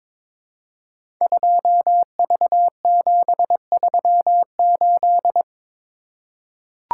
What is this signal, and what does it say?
Morse code, heard as a keyed tone, about 22 wpm, 690 Hz. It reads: 2V738
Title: Code Groups - 5 Letter:Numbers - 22wpm